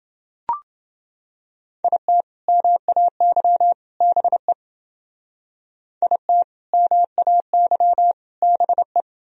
Morse code, heard as a keyed tone, about 29 wpm, 700 Hz. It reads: IT MAY BE IT MAY BE